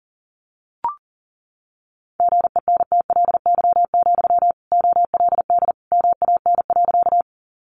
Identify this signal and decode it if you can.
Morse code, heard as a keyed tone, about 40 wpm, 700 Hz. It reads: GENTLY, OLD MAN.
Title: Sentences from Top 1500 Words 40wpm